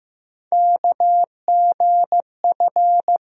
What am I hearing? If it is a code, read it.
Morse code, heard as a keyed tone, about 15 wpm, 700 Hz. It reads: KGF
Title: USA Call Sign Postfixes - 3 Letters 15wpm